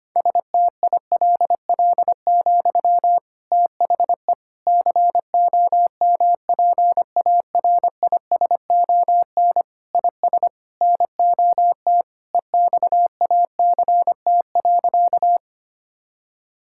Morse code, heard as a keyed tone, about 25 wpm, 700 Hz. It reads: STILL, THE COMPARISON IS NOT EXACT.
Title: Sentences from Top 1600 Words 25wpm